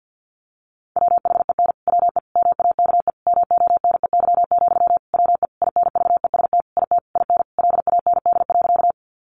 Morse code, heard as a keyed tone, about 50 wpm, 700 Hz. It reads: WHEN WE MAKE MONEY, WE INVEST IT IN LAND.
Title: Sentences from Top 1100 Words 50wpm